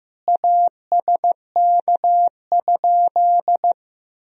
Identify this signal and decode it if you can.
Morse code, heard as a keyed tone, about 15 wpm, 700 Hz. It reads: ASK?
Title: Sentences from Top 200 Words and Abbreviations 15wpm